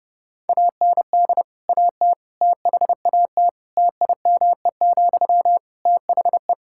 Morse code, heard as a keyed tone, about 32 wpm, 700 Hz. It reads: AND AT THAT TIME, THE
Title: Sentences from Top 1000 Words - Review Part 5 - 30wpm